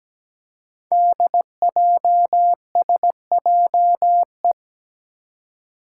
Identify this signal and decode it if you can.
Morse code, heard as a keyed tone, about 17 wpm, 700 Hz. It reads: DJSJE
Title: Five Challenging Letter-Number Combinations 17wpm